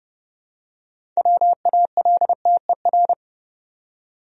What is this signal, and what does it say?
Morse code, heard as a keyed tone, about 30 wpm, 700 Hz. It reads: WALTER
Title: USA Common Names - 30wpm